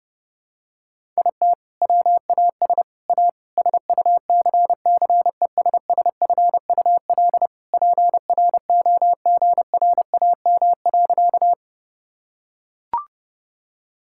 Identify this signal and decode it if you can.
Morse code, heard as a keyed tone, about 30 wpm, 700 Hz. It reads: IT WAS A SUCCESSFUL PROGRAM.
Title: Sentences from Top 1000 Words - Review Part 5 - 30wpm